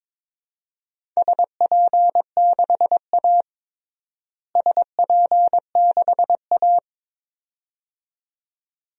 Morse code, heard as a keyed tone, about 22 wpm, 700 Hz. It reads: SP6A SP6A
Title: DXCC-100 Call Signs - 4 Characters - Repeated 2x - 22wpm